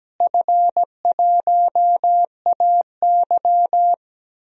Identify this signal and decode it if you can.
Morse code, heard as a keyed tone, about 17 wpm, 700 Hz. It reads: F1AY
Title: DXCC-100 Call Signs - 4 Characters - Repeated 3x - 17wpm